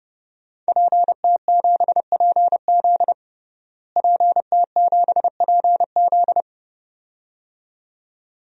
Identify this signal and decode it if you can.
Morse code, heard as a keyed tone, about 30 wpm, 700 Hz. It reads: PT7PZ PT7PZ
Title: DXCC-100 Call Signs - 5 Characters - Repeated 2x - 30wpm